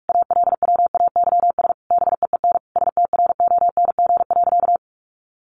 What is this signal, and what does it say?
Morse code, heard as a keyed tone, about 44 wpm, 700 Hz. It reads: ALWAYS BEEN STRONG.
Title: Sentences from Top 1400 Words 45wpm